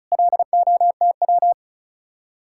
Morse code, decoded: LOTW

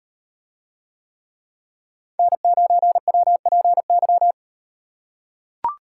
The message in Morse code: N9WPY

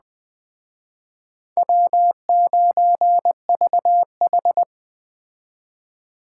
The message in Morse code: W9VH

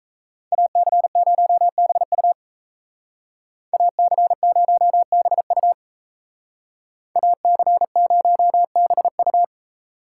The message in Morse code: AC0BU AC0BU AC0BU

